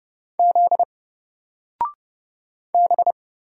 Morse code, heard as a keyed tone, about 30 wpm, 700 Hz. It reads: Z B
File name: Single-Letter-Number-30wpm